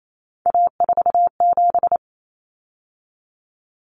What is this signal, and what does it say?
Morse code, heard as a keyed tone, about 28 wpm, 700 Hz. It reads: A47
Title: Common Cut Numbers - 3-Digit 28wpm